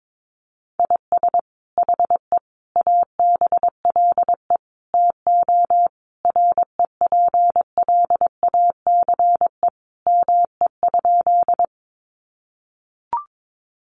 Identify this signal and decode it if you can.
Morse code, heard as a keyed tone, about 22 wpm, 700 Hz. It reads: IS HE ABLE TO REPLACE ME?
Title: Sentences from Top 1400 Words 22wpm